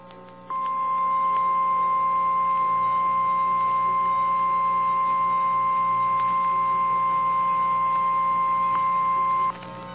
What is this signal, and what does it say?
Emergency Alert System (SAME) tones (the 1050 Hz attention tone). It is heard from 0.5 to 9.6 s.
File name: eas / KNCYKBIE / KNCYKBIE